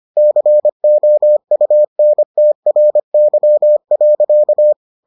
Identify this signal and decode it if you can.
Morse code, heard as a keyed tone, about 25 wpm, 600 Hz. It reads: COUNTRY.